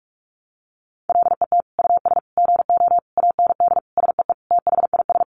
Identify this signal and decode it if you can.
Morse code, heard as a keyed tone, about 44 wpm, 700 Hz. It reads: LET US GO AND SEE THIS
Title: Sentences from Top 1100 Words 45wpm